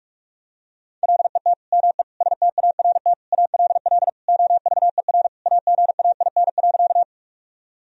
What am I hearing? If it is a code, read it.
Morse code, heard as a keyed tone, about 46 wpm, 700 Hz. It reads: LET ME START ALL OVER AGAIN.
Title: Sentences from Top 1000 Words - Review Part 1 - 45wpm